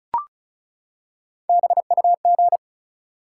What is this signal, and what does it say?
Morse code, heard as a keyed tone, about 36 wpm, 700 Hz. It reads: BUG
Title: Three Letter Words - 35wpm